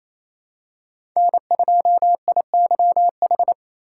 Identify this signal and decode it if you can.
Morse code, heard as a keyed tone, about 28 wpm, 700 Hz. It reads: N2IYH
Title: USA Call Signs - 5 Characters - 2x Character Spacing - 28wpm